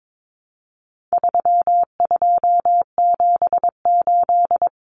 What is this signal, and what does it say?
Morse code, heard as a keyed tone, about 22 wpm, 700 Hz. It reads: 3278
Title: Common Cut Numbers - 4-Digit 22wpm